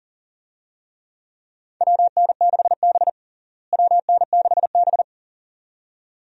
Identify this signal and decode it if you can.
Morse code, heard as a keyed tone, about 40 wpm, 700 Hz. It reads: WN6B WN6B
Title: USA Call Signs - 4 Characters - Repeated 2x - 40wpm